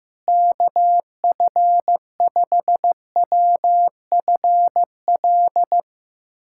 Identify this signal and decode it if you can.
Morse code, heard as a keyed tone, about 15 wpm, 700 Hz. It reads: KF5WFL
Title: USA Callsign DE Callsign 15wpm